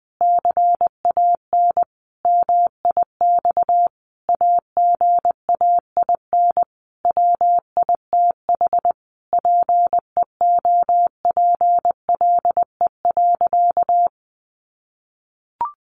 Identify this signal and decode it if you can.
Morse code, heard as a keyed tone, about 20 wpm, 700 Hz. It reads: CAN MIX AGAIN WITH PEOPLE.
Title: Sentences from Top 1100 Words 20wpm